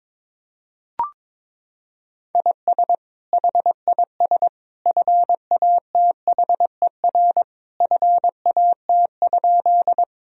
Morse code, heard as a keyed tone, about 22 wpm, 700 Hz. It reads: IS HIS FATHER FAT?
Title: Sentences from Top 1000 Words - Review Part 4 - 22wpm